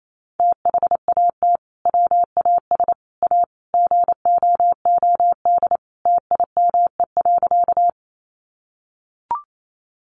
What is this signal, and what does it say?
Morse code, heard as a keyed tone, about 28 wpm, 700 Hz. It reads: THAT WAS A GOOD TIME.
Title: Sentences from Top 200 Words 28wpm